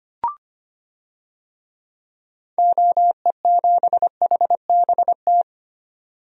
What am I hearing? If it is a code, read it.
Morse code, heard as a keyed tone, about 25 wpm, 700 Hz. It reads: OE7HBT
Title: DXCC-100 Call Signs - 6 Characters - 25wpm